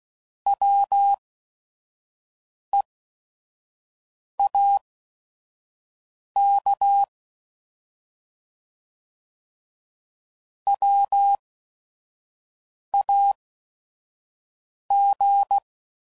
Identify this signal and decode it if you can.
Morse code, heard as a keyed tone, about 16 wpm, spaced as for 5 wpm, 800 Hz. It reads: WEAK WAG